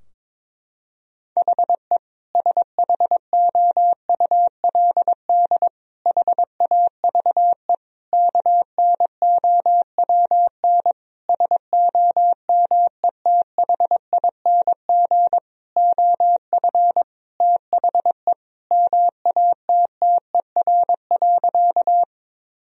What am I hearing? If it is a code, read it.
Morse code, heard as a keyed tone, about 22 wpm, 700 Hz. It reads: HE SHOULD HAVE KNOWN SOMETHING OF THE MATTER.
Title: Sentences from Top 1000 Words - Review Part 3 - 22wpm